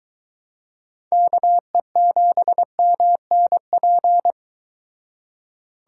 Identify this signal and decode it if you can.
Morse code, heard as a keyed tone, about 24 wpm, 700 Hz. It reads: KE7MNP